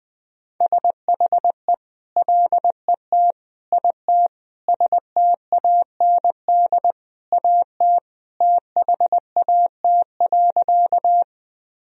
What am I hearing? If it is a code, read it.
Morse code, heard as a keyed tone, about 21 wpm, 700 Hz. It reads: SHE LET IT STAND AT THAT.
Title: Sentences from Top 1000 Words - Review Part 1 - 20wpm